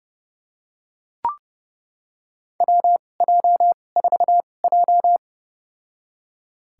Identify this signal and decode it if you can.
Morse code, heard as a keyed tone, about 31 wpm, spaced as for 22 wpm, 700 Hz. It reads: WJ4J
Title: USA Call Signs - 4 Characters - 2x Character Spacing - 30wpm